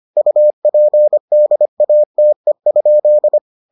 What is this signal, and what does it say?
Morse code, heard as a keyed tone, about 25 wpm, 600 Hz. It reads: UPDATE?